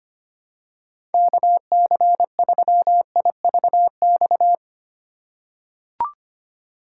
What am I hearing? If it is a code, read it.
Morse code, heard as a keyed tone, about 25 wpm, 700 Hz. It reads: KC3IVX
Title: USA Call Signs - 6 Characters - 6x Character Spacing - 25wpm